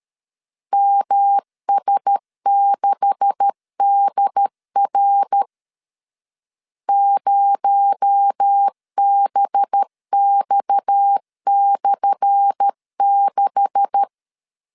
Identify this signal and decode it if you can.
Morse code, heard as a keyed tone, about 12 wpm, 800 Hz. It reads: MS6DR 0BX/6
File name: LEZIONI / 10 SIGNS / 10C DOT 12wpm 2451